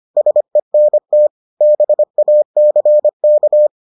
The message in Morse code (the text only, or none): SENT BACK